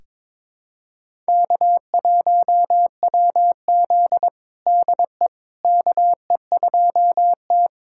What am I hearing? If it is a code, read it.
Morse code, heard as a keyed tone, about 22 wpm, 700 Hz. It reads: K1WZ DE KE2T
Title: USA Callsign DE Callsign - 4 Characters - 22wpm